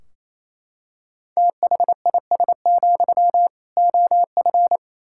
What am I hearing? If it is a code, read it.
Morse code, heard as a keyed tone, about 28 wpm, 700 Hz. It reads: THIS, OF
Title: Sentences from Top 1600 Words 28wpm